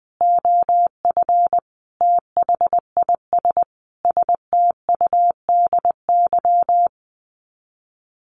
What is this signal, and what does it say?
Morse code, heard as a keyed tone, about 20 wpm, 700 Hz. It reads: OF THIS STUDY
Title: Sets of 5 Words 20wpm